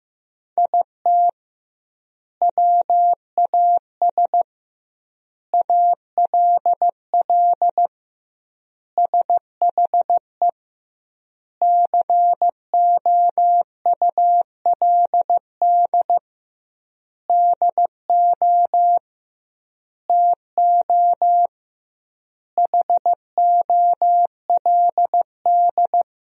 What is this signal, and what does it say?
Morse code, heard as a keyed tone, about 15 wpm, 700 Hz. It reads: IT WAS ALL SHE COULD DO TO HOLD
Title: 2x Word-Spacing - Sentences from Top 400 Words - 15wpm